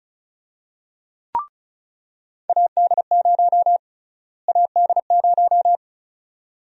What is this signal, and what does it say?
Morse code, heard as a keyed tone, about 36 wpm, 700 Hz. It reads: AD0 AD0
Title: USA Call Sign Prefixes - Repeated 2x - 35wpm